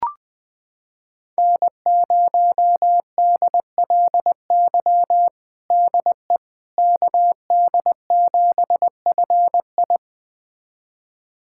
Morse code, heard as a keyed tone, about 20 wpm, 700 Hz. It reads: N0DLY DE KD7FI
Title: USA Callsign DE Callsign - 5 Characters - 20wpm